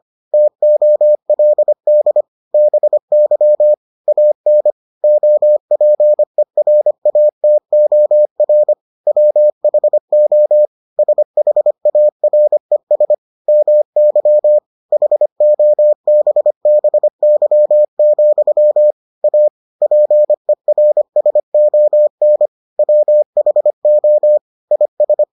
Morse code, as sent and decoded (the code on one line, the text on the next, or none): - --- .-.. -.. / -... -.-- / .- -. / --- .--. .-. .- - --- .-. / .-- .... --- / ... .... .- .-. ... / -- -.-- / .... --- -... -... -.-- --..-- / .- / .--. .-. ... --- -. / .-- .... --- / .. ...
TOLD BY AN OPERATOR WHO SHARES MY HOBBY, A PERSON WHO IS